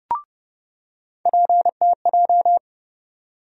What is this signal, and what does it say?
Morse code, heard as a keyed tone, about 30 wpm, 700 Hz. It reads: PTJ
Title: USA Call Sign Postfixes - 3 Letters 30wpm